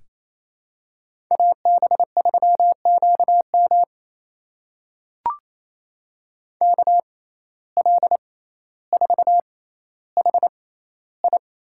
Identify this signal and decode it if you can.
Morse code, heard as a keyed tone, about 28 wpm, spaced as for 14 wpm, 700 Hz. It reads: AB3QM KL4HI
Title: USA Call Signs - 6x Character Spacing - 28wpm